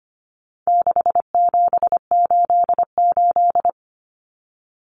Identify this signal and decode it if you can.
Morse code, heard as a keyed tone, about 25 wpm, 700 Hz. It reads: 6788
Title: Numbers - 4-Digit 25wpm